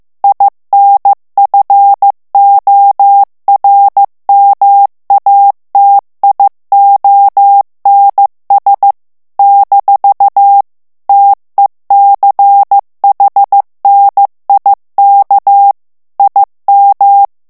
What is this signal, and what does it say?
Morse code, heard as a keyed tone, about 15 wpm, 800 Hz. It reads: INFORMATIONS - TECHNIK IM
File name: ALLEMAND texte clair 1 vitesse 14 mots minute